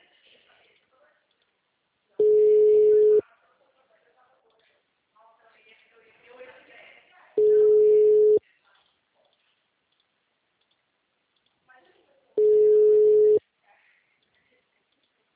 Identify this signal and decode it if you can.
Ringback tone (Europe, 425 Hz), with speech.
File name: out- -2039- - - .16643